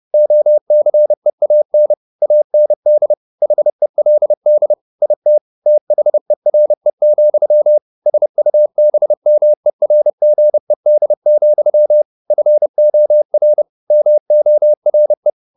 Morse code, heard as a keyed tone, about 30 wpm, 600 Hz. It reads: OCEAN AND HELD IT THERE, SUBMERGED, FOR MORE
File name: the old man and the sea / the old man and the sea 1 30wpm